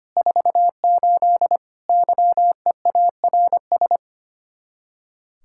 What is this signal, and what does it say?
Morse code, heard as a keyed tone, about 33 wpm, 700 Hz. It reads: 48 YEARS